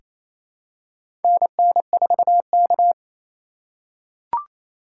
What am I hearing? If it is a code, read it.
Morse code, heard as a keyed tone, about 28 wpm, 700 Hz. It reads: NN4K
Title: USA Call Signs - 2x Character Spacing - 28wpm